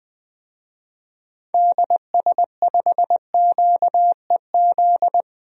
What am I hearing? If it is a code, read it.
Morse code, heard as a keyed tone, about 20 wpm, 700 Hz. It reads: DS5QEZ
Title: DXCC-100 Call Signs - 6 Characters - 20wpm